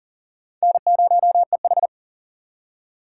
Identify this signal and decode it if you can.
Morse code, heard as a keyed tone, about 53 wpm, spaced as for 47 wpm, 700 Hz. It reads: N0EH
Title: Speed Racing - USA 1x2 Call Signs 40wpm